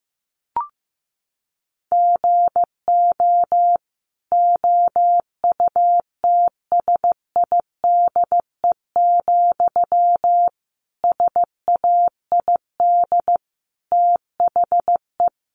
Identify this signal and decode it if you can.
Morse code, heard as a keyed tone, about 15 wpm, 700 Hz. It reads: GO OUTSIDE, SAID THE
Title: Sentences from Top 900 Words 15wpm